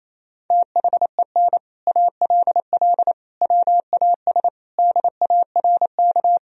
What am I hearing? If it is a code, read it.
Morse code, heard as a keyed tone, about 28 wpm, 700 Hz. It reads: THEN ALL WAS DARK
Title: Sentences from Top 500 Words and Abbreviations 28wpm